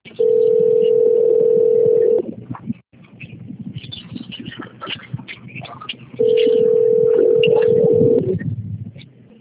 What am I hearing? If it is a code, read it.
Ringback tone (North America), 440 + 480 Hz.